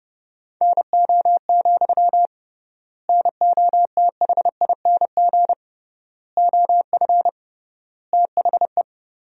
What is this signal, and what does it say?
Morse code, heard as a keyed tone, about 30 wpm, 700 Hz. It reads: NO, NOTHING OF THE
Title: 3x Word-Spacing - Sentences from Top 500 Words - 30wpm